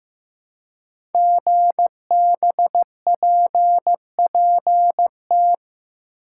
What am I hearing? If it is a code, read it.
Morse code, heard as a keyed tone, about 15 wpm, 700 Hz. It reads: GBPPT
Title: Code Groups - 5 Letters - 15wpm